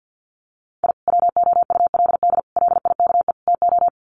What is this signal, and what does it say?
Morse code, heard as a keyed tone, about 50 wpm, 700 Hz. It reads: I WOULD LIKE TO
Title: Sets of 4 Words 50wpm